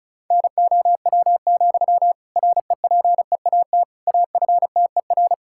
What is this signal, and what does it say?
Morse code, heard as a keyed tone, about 36 wpm, 700 Hz. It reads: NOW, REPEAT AFTER
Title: Sentences from Top 1600 Words 35wpm